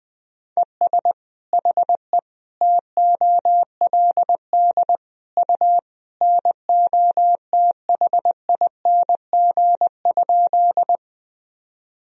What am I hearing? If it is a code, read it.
Morse code, heard as a keyed tone, about 20 wpm, 700 Hz. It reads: ES HE TOLD U NOTHING?
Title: Sentences from Top 500 Words and Abbreviations 20wpm